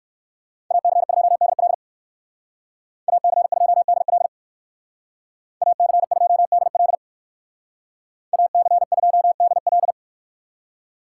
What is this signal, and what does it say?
Morse code, heard as a keyed tone, about 58 wpm, 700 Hz. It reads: AC2DL AC2DL AC2DL AC2DL